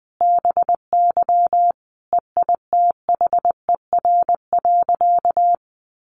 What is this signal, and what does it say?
Morse code, heard as a keyed tone, about 20 wpm, 700 Hz. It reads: BY EITHER.
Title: Sentences from Top 600 Words 20wpm